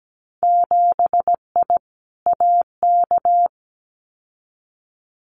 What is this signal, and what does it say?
Morse code, heard as a keyed tone, about 17 wpm, 700 Hz. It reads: 7I AK
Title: Winter Field Day Exchanges Encore 17wpm